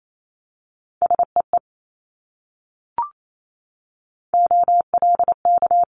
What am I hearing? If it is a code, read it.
Morse code, heard as a keyed tone, about 28 wpm, 700 Hz. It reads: SEE OLK
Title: Multi-letter Phonemes Repeated 3x 28wpm